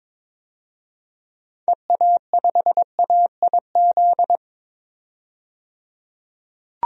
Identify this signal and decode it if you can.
Morse code, heard as a keyed tone, about 21 wpm, 700 Hz. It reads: EA5AIZ
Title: DXCC-100 Call Signs - 6 Characters - 22wpm